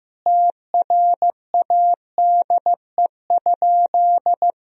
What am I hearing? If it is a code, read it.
Morse code, heard as a keyed tone, about 15 wpm, 700 Hz. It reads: TRADE?
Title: Sentences from Top 1000 Words - Review Part 4 - 15wpm